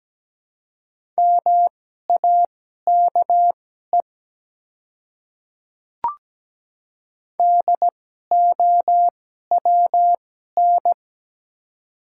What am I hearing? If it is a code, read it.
Morse code, heard as a keyed tone, about 17 wpm, spaced as for 12 wpm, 700 Hz. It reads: MAKE DOWN
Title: 2x Character Spacing - Top 100 Words - 17wpm